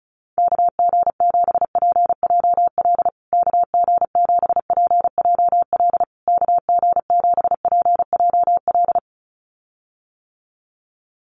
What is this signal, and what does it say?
Morse code, heard as a keyed tone, about 34 wpm, 700 Hz. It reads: KG7PJL KG7PJL KG7PJL
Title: USA Call Signs Repeated 3x 35wpm